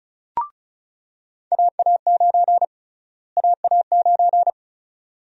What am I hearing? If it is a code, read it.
Morse code, heard as a keyed tone, about 34 wpm, 700 Hz. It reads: AA9 AA9
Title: USA Call Sign Prefixes - Repeated 2x - 35wpm